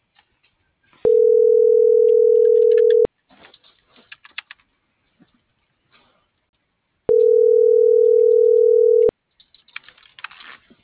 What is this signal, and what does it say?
Ringback tone (North America), 440 + 480 Hz.
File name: CENTRAL MULTIMODAL / out viridiana.juarez - .83224